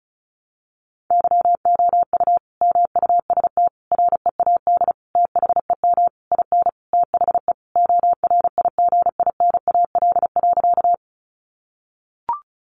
Morse code, heard as a keyed tone, about 34 wpm, 700 Hz. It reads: YOU MUST READ THEM IN THE ORIGINAL.
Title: Sentences from Top 900 Words 35wpm